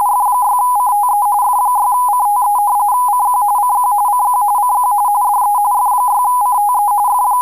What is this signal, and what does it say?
RTTY, about 45.45 baud. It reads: F6CTE F6CTE CQ CQ CQ CQ DE F6CTE F